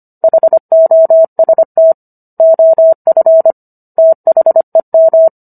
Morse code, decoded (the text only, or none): HOST OF THEM